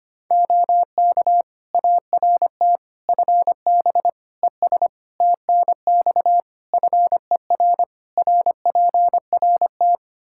Morse code, heard as a keyed tone, about 25 wpm, 700 Hz. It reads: OK ART FB ES TNX FER RPRT
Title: Practice QSO - Exchange 2 - Info - 25wpm